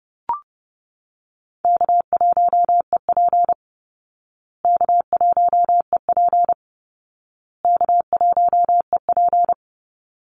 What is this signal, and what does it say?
Morse code, heard as a keyed tone, about 28 wpm, 700 Hz. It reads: K1EP K1EP K1EP
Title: Easy Call Signs Repeated 3x - 30wpm